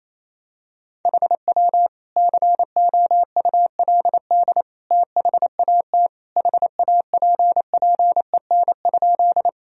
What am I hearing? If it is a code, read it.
Morse code, heard as a keyed tone, about 28 wpm, 700 Hz. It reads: HW COULD THAT HAPPEN?
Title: Sentences from Top 400 Words and Abbreviations 28wpm